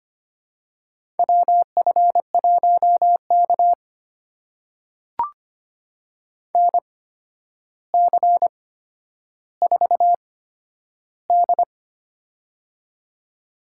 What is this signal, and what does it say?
Morse code, heard as a keyed tone, about 25 wpm, spaced as for 11 wpm, 700 Hz. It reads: WF1K NC4D
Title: USA Call Signs - 4 Characters - 8x Character Spacing - 25wpm